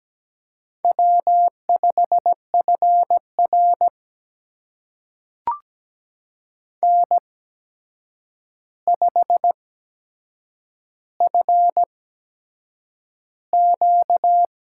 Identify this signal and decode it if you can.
Morse code, heard as a keyed tone, about 17 wpm, spaced as for 7 wpm, 700 Hz. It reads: W5FR N5FQ
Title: USA Call Signs - 4 Characters - 8x Character Spacing - 17wpm